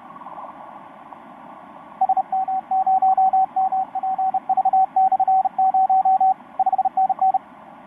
Morse code, heard as a keyed tone, about 32 wpm, 760 Hz, with radio noise. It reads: SM0MPV/0 5NN
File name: vk9-oh1vr ba7io etc 10m cw 22nov2012 0827